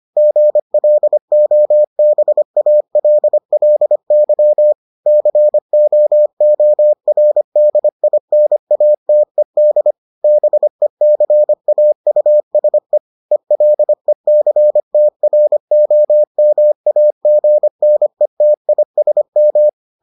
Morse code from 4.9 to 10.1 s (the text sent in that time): COORDINATED